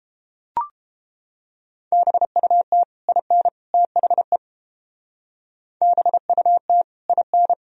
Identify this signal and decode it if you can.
Morse code, heard as a keyed tone, about 29 wpm, 700 Hz. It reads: BUT IN THE BUT IN